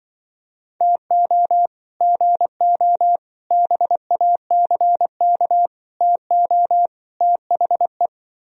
Morse code, heard as a keyed tone, about 23 wpm, 700 Hz. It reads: TO GO BACK TO THE
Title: Speed-Racing - Sets of 5 Words 20wpm